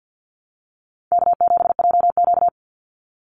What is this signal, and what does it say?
Morse code, heard as a keyed tone, about 49 wpm, 700 Hz. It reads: K7JQ
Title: Call Signs Repeated 3x - 50wpm